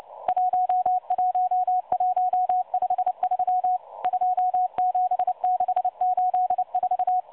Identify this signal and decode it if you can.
Morse code, heard as a keyed tone, about 29 wpm, 730 Hz. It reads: 11153 27684
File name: M12 - 8116 kHz CW 2-11-14